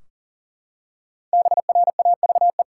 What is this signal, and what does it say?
Morse code, heard as a keyed tone, about 40 wpm, 700 Hz. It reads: BRAVE
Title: Five Letter Words - 40wpm